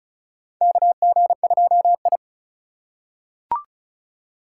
Morse code, heard as a keyed tone, about 36 wpm, 690 Hz. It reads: KG2I
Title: USA Call Signs - 4 Characters - 3x Character Spacing - 35wpm